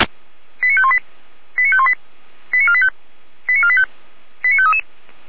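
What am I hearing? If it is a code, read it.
Five-tone selective call (ZVEI), heard as quick runs of five tones, tones of about 70 ms. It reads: 89518 89518 89485 89485 89533
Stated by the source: Five-tone ZVEI selective call received on SDR.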